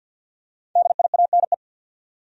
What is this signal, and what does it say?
Morse code, heard as a keyed tone, about 47 wpm, 700 Hz. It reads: DIANE
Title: USA Common Names - 50wpm